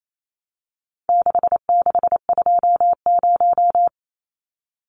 Morse code, heard as a keyed tone, about 28 wpm, 700 Hz. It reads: 6620
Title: Numbers - 4-Digit 28wpm